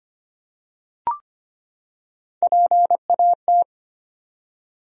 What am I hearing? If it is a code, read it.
Morse code, heard as a keyed tone, about 25 wpm, 700 Hz. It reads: PAT
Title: USA Common Names - Max 3 Letters - 25wpm